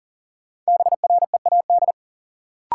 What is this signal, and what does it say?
Morse code, heard as a keyed tone, about 40 wpm, 700 Hz. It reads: BREAD
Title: Sentences from Top 1600 Words - New Word Drill Alt - 40wpm